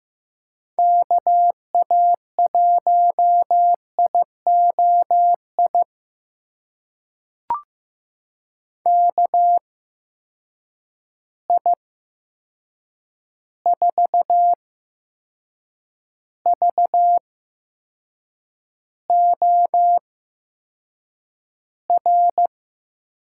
Morse code, heard as a keyed tone, about 15 wpm, spaced as for 6 wpm, 700 Hz. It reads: KA1IOI KI4VOR
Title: USA Call Signs - 6 Characters - 8x Character Spacing - 15wpm